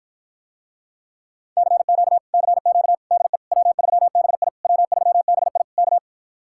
Morse code, heard as a keyed tone, about 54 wpm, 700 Hz. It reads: CQ CQ DE W3BI W3BI K